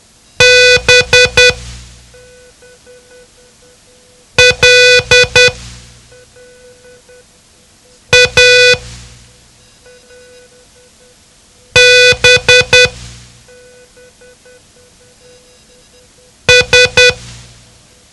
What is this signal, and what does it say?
Morse code, heard as a keyed tone, about 10 wpm, spaced as for 3 wpm, 500 Hz. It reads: BLABS
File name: ham radio / CodeQuick / Code Quick 07